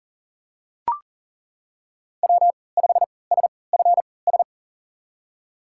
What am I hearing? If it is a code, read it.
Morse code, heard as a keyed tone, about 38 wpm, spaced as for 22 wpm, 700 Hz. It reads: W5SFS
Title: USA Call Signs - 5 Characters - 3x Character Spacing - 40wpm